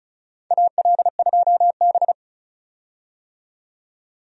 Morse code, heard as a keyed tone, about 36 wpm, 700 Hz. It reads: AL2B
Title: USA Call Signs - 4 Characters - 35wpm